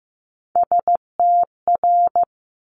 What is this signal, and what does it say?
Morse code, heard as a keyed tone, about 15 wpm, 700 Hz. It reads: STR
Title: Top Three Letter Combinations in Words 15wpm